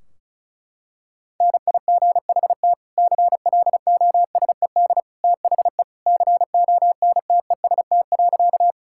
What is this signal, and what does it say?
Morse code, heard as a keyed tone, about 36 wpm, 700 Hz. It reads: NIGHT CLOSED THE CONTEST.
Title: Sentences from Top 1600 Words 35wpm